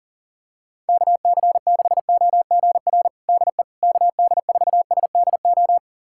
Morse code, heard as a keyed tone, about 37 wpm, 700 Hz. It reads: KC6OGR DE KD4SDO
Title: USA Callsign DE Callsign 40wpm